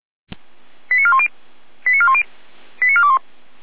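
Five-tone selective call (ZVEI), heard as quick runs of five tones, tones of about 70 ms. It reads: 89410 89410 89421
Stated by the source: ZVEI five-tone call received via SDR.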